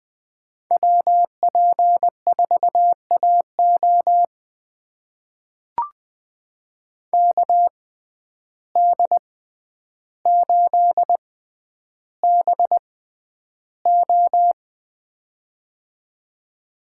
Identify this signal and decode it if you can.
Morse code, heard as a keyed tone, about 21 wpm, spaced as for 10 wpm, 700 Hz. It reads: WP4AO KD8BO
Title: USA Call Signs - 5 Characters - 6x Character Spacing - 20wpm